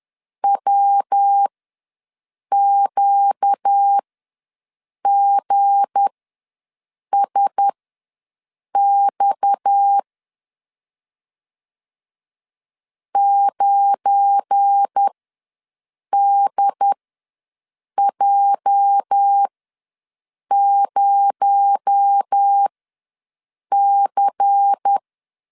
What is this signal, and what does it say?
Morse code, heard as a keyed tone, about 10 wpm, spaced as for 6 wpm, 800 Hz. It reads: WQGSX 9DJ0C
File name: LEZIONI / 8 YJ45 / 8 YJ ws 10wpm 4043